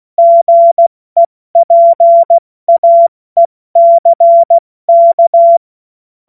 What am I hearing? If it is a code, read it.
Morse code, heard as a keyed tone, about 16 wpm, spaced as for 14 wpm, 680 Hz. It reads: GEPAECK
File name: Morsekurs / Witze / Witz 16 Januar 2024 70 F80